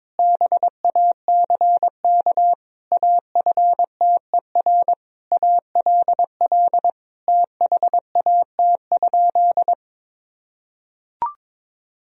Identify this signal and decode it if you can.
Morse code, heard as a keyed tone, about 22 wpm, 700 Hz. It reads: BACK AFTER ALL THAT?
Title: Sentences from Top 1000 Words - Review Part 1 - 22wpm